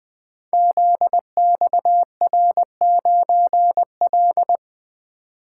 Morse code, heard as a keyed tone, about 20 wpm, 700 Hz. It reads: ZXR9L
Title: Code Groups - 5 Letter:Numbers - 20wpm